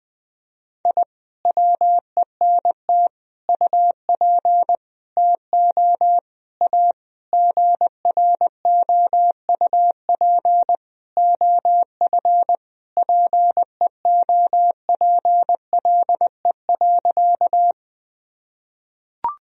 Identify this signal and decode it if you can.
Morse code, heard as a keyed tone, about 20 wpm, 700 Hz. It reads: I WENT UP TO A GROUP OF PEOPLE.
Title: Sentences from Top 1000 Words - Review Part 1 - 20wpm